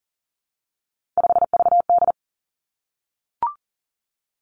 Morse code, heard as a keyed tone, about 40 wpm, 700 Hz. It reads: 5VD
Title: Three Challenging Letter-Number Combinations 40wpm